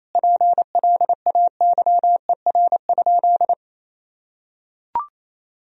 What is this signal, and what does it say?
Morse code, heard as a keyed tone, about 28 wpm, 700 Hz. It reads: PLAYER?
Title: Sentences from Top 1000 Words - Review Part 4 - 28wpm